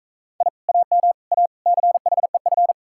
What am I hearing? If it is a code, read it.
Morse code, heard as a keyed tone, about 41 wpm, 700 Hz. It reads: I AM A CHEF